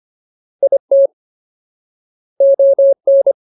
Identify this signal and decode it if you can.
Morse code, heard as a keyed tone, about 25 wpm, 550 Hz. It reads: IT ON
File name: Ultimate Word List / 25WPM Ultimate